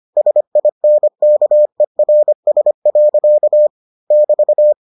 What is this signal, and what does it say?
Morse code, heard as a keyed tone, about 25 wpm, 600 Hz. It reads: SINKERS. <BT>